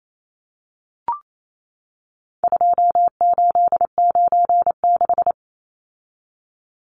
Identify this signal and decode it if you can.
Morse code, heard as a keyed tone, about 28 wpm, 700 Hz. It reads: 2896
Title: Numbers - 4-Digit 28wpm